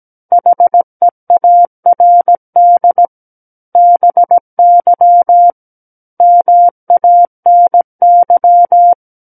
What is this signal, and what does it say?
Morse code, heard as a keyed tone, about 17 wpm, 700 Hz. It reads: HEARD BY MANY
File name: Book One / 08-Chapter 8 - 17 WPM